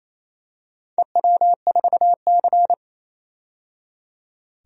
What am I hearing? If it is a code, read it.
Morse code, heard as a keyed tone, about 28 wpm, 700 Hz. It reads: EW4C
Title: DXCC-100 Call Signs - 4 Characters - 28wpm